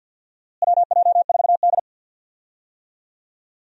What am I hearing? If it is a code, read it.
Morse code, heard as a keyed tone, about 50 wpm, 700 Hz. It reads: WJ4D
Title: USA Call Signs - 4 Characters - Repeated 3x - 50wpm